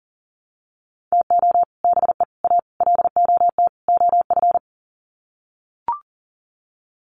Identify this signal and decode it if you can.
Morse code, heard as a keyed tone, about 40 wpm, 700 Hz. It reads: TO BE A LOT OF
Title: Sets of 5 Words Encore 40wpm